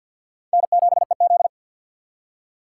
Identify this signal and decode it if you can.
Morse code, heard as a keyed tone, about 49 wpm, 700 Hz. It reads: N7EZ
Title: USA Call Signs - 4 Characters - 3x Character Spacing - 50wpm